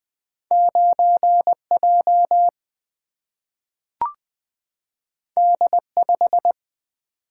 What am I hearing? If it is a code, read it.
Morse code, heard as a keyed tone, about 20 wpm, 700 Hz. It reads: 9J D5
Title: Single-Letter Single-Number 20wpm